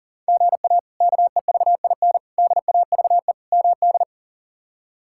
Morse code, heard as a keyed tone, about 40 wpm, 700 Hz. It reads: GA KEVIN DAVE MD
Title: K1USN SST QSO Practice - 40wpm